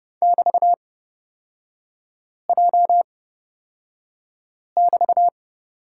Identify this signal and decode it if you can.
Morse code, heard as a keyed tone, about 30 wpm, spaced as for 5 wpm, 700 Hz. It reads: <BT>J<BT>
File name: Lesson 36e - BT - Three Characters 30wpm